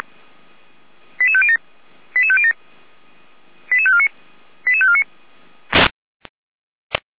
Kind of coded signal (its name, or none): five-tone selective call